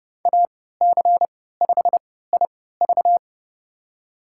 Morse code, heard as a keyed tone, about 30 wpm, spaced as for 17 wpm, 700 Hz. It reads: AC5IV
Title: USA Call Signs - 5 Characters - 3x Character Spacing - 30wpm